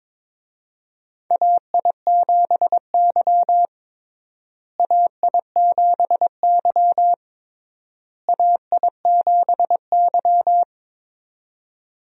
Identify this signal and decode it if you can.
Morse code, heard as a keyed tone, about 22 wpm, 700 Hz. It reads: AI7Y AI7Y AI7Y
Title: USA Call Signs - 4 Characters - Repeated 3x - 22wpm